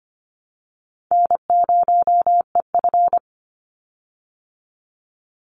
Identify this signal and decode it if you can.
Morse code, heard as a keyed tone, about 25 wpm, 700 Hz. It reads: N0EF
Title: Easy Call Signs Repeated 2x - 25wpm